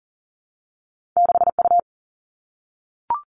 Morse code, heard as a keyed tone, about 42 wpm, 700 Hz. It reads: 6U
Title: All Cut Numbers - 2-Digit 40wpm